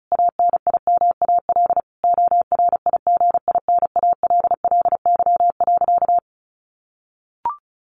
Morse code, heard as a keyed tone, about 34 wpm, 700 Hz. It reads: ANIMAL ORIGINALLY.
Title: A Sound Of Thunder 35wpm